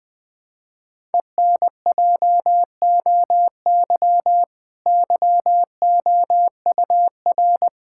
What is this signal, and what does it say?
Morse code, heard as a keyed tone, about 21 wpm, 700 Hz. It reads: ENJOY YOUR
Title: Sentences from Top 1000 Words - Review Part 5 - 20wpm